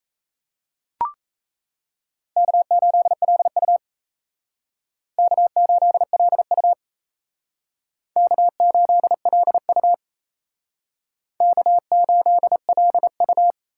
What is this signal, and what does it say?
Morse code, heard as a keyed tone, about 37 wpm, 700 Hz. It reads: K8LU K8LU K8LU K8LU